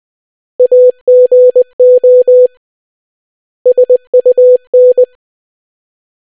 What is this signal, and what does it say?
Morse code, heard as a keyed tone, about 20 wpm, 500 Hz. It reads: AGO SUN